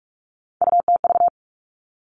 Morse code, heard as a keyed tone, about 46 wpm, 700 Hz. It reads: UTV
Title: Three Challenging Letter-Number Combinations 45wpm